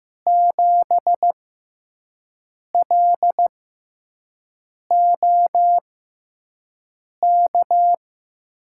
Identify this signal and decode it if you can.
Morse code, heard as a keyed tone, about 15 wpm, spaced as for 5 wpm, 700 Hz. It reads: 7LOK
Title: USA Call Signs - 6 Characters - 6x Character Spacing - 15wpm